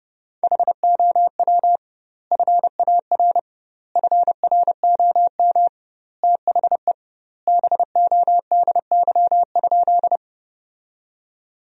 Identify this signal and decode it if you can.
Morse code, heard as a keyed tone, about 32 wpm, 700 Hz. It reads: HOW FAR FROM THE BODY?
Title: 2x Word-Spacing - Sentences from Top 400 Words - 30wpm